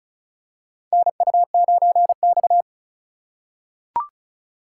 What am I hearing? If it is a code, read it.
Morse code, heard as a keyed tone, about 34 wpm, 700 Hz. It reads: NU9X
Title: USA Call Signs - 4 Characters - 8x Character Spacing - 35wpm